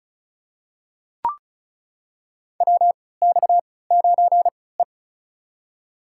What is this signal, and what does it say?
Morse code, heard as a keyed tone, about 36 wpm, spaced as for 20 wpm, 700 Hz. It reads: WX9E
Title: USA Call Signs - 4 Characters - 3x Character Spacing - 35wpm